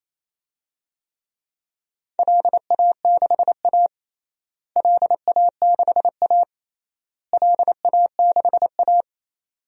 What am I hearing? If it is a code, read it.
Morse code, heard as a keyed tone, about 28 wpm, 700 Hz. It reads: LA6A LA6A LA6A
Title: DXCC-100 Call Signs - 4 Characters - Repeated 3x - 28wpm